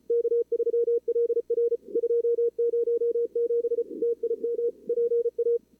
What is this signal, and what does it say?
Morse code, heard as a keyed tone, about 31 wpm, 440 Hz. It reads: K3LR 207 TIM PA
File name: CWSprint / Feb2022 / k5tr-feb-na-sprint-cw-k5ot -0100